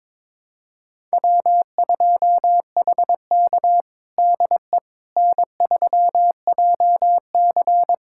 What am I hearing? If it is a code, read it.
Morse code, heard as a keyed tone, about 22 wpm, 700 Hz. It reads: W2HK DE N3JC
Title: USA Callsign DE Callsign - 4 Characters - 22wpm